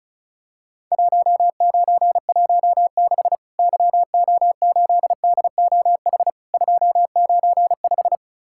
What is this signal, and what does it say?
Morse code, heard as a keyed tone, about 36 wpm, 700 Hz. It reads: 1916 YO8DOH 295
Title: Worked All Europe DX Contest - QTC Exchange - 35wpm